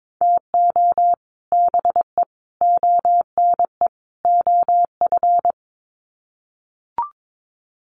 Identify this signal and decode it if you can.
Morse code, heard as a keyed tone, about 22 wpm, 700 Hz. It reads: TO BE ONE OF
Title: Sets of 4 Words 22wpm